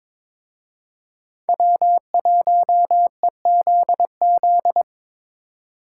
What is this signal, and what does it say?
Morse code, heard as a keyed tone, about 25 wpm, 700 Hz. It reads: W1EZZ
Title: Speed Racing - USA 1x3 Call Signs 22wpm